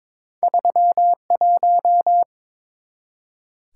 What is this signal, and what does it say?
Morse code, heard as a keyed tone, about 22 wpm, 690 Hz. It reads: 31
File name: practice / 22wpm / Two Letter-Number Combinations - 22wpm